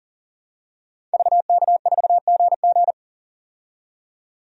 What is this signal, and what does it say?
Morse code, heard as a keyed tone, about 40 wpm, 700 Hz. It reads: VK4GG
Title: DXCC-100 Call Signs - 5 Characters - 40wpm